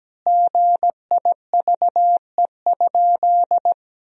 Morse code, heard as a keyed tone, about 17 wpm, 700 Hz. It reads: GIVE?
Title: 3x Word-Spacing - Sentences from Top 200 Words - 17wpm